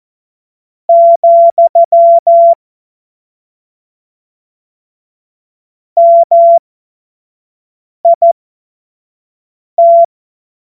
Morse code, heard as a keyed tone, about 14 wpm, spaced as for 5 wpm, 680 Hz. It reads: , MIT
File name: Morsekurs / Witze / Witz 05. Oktober 2021, 25 F70